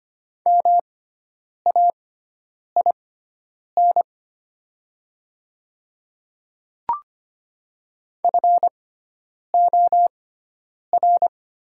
Morse code, heard as a keyed tone, about 25 wpm, spaced as for 9 wpm, 700 Hz. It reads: MAIN FOR
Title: 6x Character Spacing - Top 300 Words - 25wpm